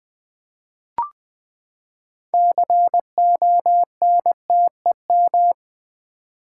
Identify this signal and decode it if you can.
Morse code, heard as a keyed tone, about 19 wpm, 700 Hz. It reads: CONTEM
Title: Build A Word - 6 Letters At A Time - 12-Letter Words - 20wpm